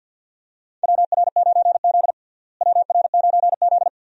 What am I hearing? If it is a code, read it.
Morse code, heard as a keyed tone, about 48 wpm, 700 Hz. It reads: WR9Z WR9Z
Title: USA Call Signs - 4 Characters - Repeated 2x - 50wpm